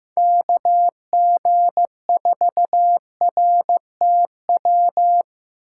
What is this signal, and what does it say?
Morse code, heard as a keyed tone, about 16 wpm, 700 Hz. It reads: KG4RTW
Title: USA Call Signs - 6 Characters - 4x Character Spacing - 15wpm